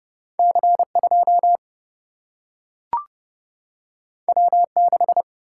Morse code, heard as a keyed tone, about 30 wpm, 700 Hz. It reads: C 2 W 6
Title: Single-Letter Single-Number 30wpm